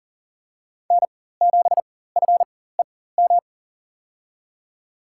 Morse code, heard as a keyed tone, about 40 wpm, spaced as for 19 wpm, 700 Hz. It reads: N7FEM
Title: USA Call Signs - 4x Character Spacing - 40wpm